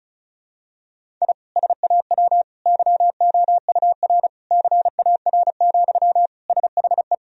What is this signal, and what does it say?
Morse code, heard as a keyed tone, about 36 wpm, 700 Hz. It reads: I SAW YOUR CAR, SHE
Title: Sentences from Top 1000 Words - Review Part 1 - 35wpm